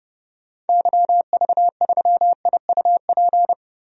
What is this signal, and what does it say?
Morse code, heard as a keyed tone, about 30 wpm, 700 Hz. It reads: YV3IUP
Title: DXCC-100 Call Signs - 6 Characters - Repeated 3x - 30wpm